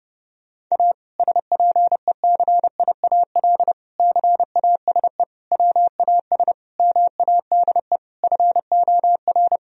Morse code, heard as a keyed tone, about 30 wpm, 700 Hz. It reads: A SPECIAL CASE WAS MADE FOR
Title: Sentences from Top 1000 Words - Review Part 3 - 30wpm